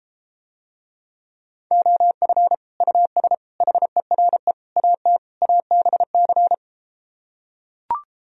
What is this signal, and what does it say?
Morse code, heard as a keyed tone, about 33 wpm, 700 Hz. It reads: OF US HERE AT ABC